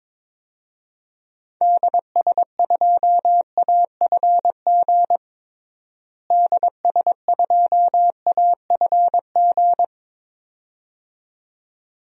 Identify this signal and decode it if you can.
Morse code, heard as a keyed tone, about 22 wpm, 700 Hz. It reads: DS2AFG DS2AFG
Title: DXCC-100 Call Signs - 6 Characters - Repeated 2x - 22wpm